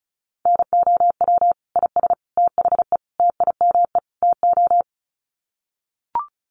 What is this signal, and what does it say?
Morse code, heard as a keyed tone, about 34 wpm, 700 Hz. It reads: NOW IS THE TIME TO
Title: Sets of 5 Words Encore2 35wpm